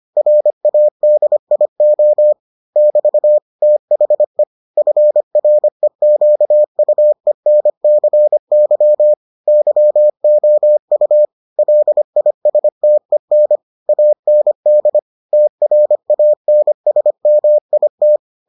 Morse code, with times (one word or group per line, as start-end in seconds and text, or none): 0.2-2.3 RADIO
2.8-3.4 <BT>
3.6-4.4 THE
4.8-9.1 FREQUENCY
9.5-11.3 YOU
11.6-13.6 LISTEN
13.9-15.0 AND
15.3-18.2 TRANSMIT